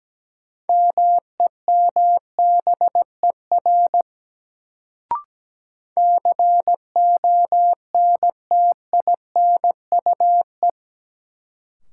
Morse code, heard as a keyed tone, about 17 wpm, 700 Hz. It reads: MEMBER CONTINUE
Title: Sentences from Top 1100 Words - New Word Drill Alt - 17wpm